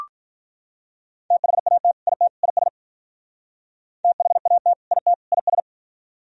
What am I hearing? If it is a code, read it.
Morse code, heard as a keyed tone, about 44 wpm, 700 Hz. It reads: THAT IT IS THAT IT IS
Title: Speed-Racing - Sets of 3 Words 35wpm